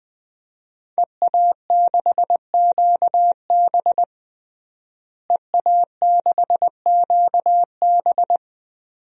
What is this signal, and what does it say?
Morse code, heard as a keyed tone, about 20 wpm, 700 Hz. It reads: EA6QB EA6QB
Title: DXCC-100 Call Signs - 5 Characters - Repeated 2x - 20wpm